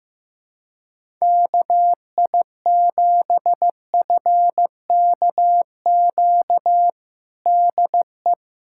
Morse code, heard as a keyed tone, about 15 wpm, 700 Hz. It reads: KI7FKQ DE
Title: USA Callsign DE Callsign - 6 Characters - 15wpm